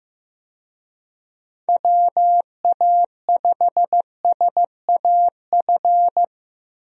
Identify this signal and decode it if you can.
Morse code, heard as a keyed tone, about 15 wpm, 700 Hz. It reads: WA5SAF
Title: USA Call Signs - 6 Characters - Repeated 2x - 15wpm